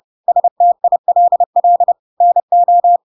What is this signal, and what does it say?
Morse code, heard as a keyed tone, about 30 wpm, 700 Hz. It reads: STILL NO
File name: Pinocchio / Pinocchio ch1 30 30 ik1hge